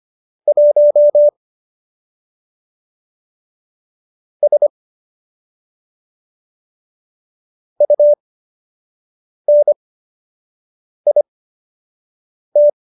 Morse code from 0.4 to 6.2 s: 1 S

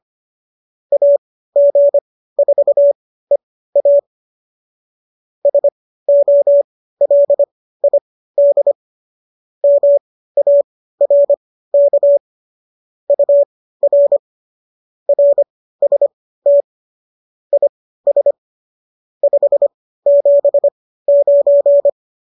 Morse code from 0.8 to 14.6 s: AG4EA SOLID MARK UR

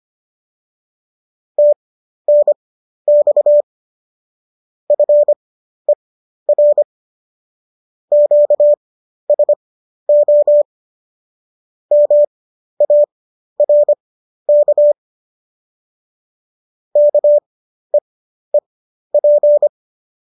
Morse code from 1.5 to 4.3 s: TNX

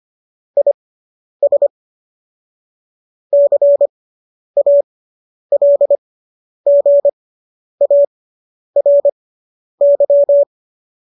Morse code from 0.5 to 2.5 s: IS

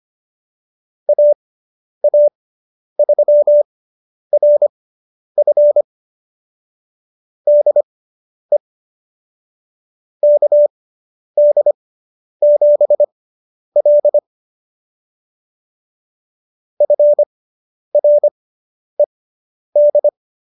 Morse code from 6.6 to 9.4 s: DE

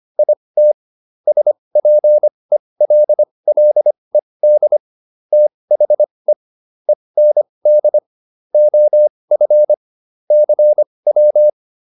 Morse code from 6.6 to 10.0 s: END OF

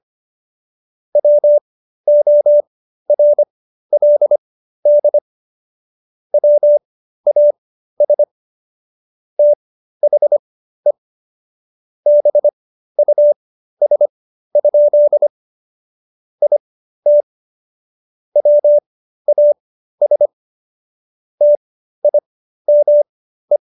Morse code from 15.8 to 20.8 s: IT WAS

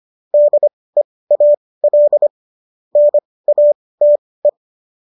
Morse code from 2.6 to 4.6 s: NATE

